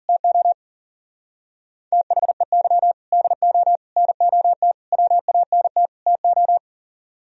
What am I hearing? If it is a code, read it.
Morse code, heard as a keyed tone, about 50 wpm, 700 Hz. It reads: TO THEY DO NOT WANT TO
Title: Speed-Racing - Sets of 5 Words 40wpm